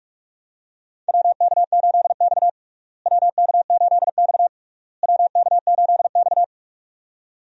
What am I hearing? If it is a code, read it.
Morse code, heard as a keyed tone, about 46 wpm, 700 Hz. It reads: WK8X WK8X WK8X
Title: USA Call Signs - 4 Characters - Repeated 3x - 45wpm